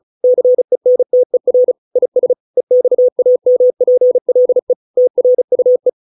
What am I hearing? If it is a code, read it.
Morse code, heard as a keyed tone, about 35 wpm, 500 Hz. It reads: CENTER IS EXAMPLE TRUE